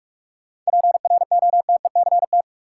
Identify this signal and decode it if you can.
Morse code, heard as a keyed tone, about 46 wpm, 700 Hz. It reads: PROTECT
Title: Sentences from Top 900 Words - New Word Drill - 45wpm